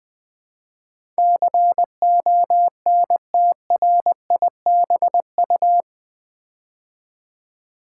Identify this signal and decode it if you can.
Morse code, heard as a keyed tone, about 20 wpm, 700 Hz. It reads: CONTRIBU
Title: Build A Word - 4 Letters At A Time - 12-Letter Words - 20wpm